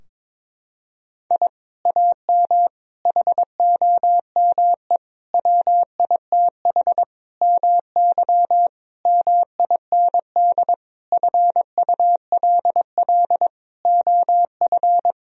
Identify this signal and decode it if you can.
Morse code, heard as a keyed tone, about 22 wpm, 700 Hz. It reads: I AM HOME WITH MY MIND FULL OF
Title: Sentences from Top 1000 Words - Review Part 3 - 22wpm